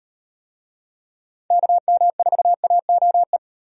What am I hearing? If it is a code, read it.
Morse code, heard as a keyed tone, about 35 wpm, 700 Hz. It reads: KM4AOE